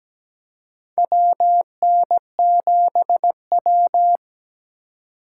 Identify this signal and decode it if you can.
Morse code, heard as a keyed tone, about 17 wpm, 700 Hz. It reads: WN7W
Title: USA Call Signs - 4 Characters - 3x Character Spacing - 17wpm